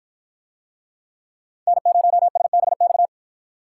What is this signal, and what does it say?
Morse code, heard as a keyed tone, about 48 wpm, 700 Hz. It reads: N0SBX